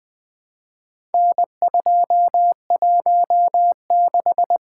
Morse code, heard as a keyed tone, about 20 wpm, 700 Hz. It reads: N216
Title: Four Letter-Number Combinations - 20wpm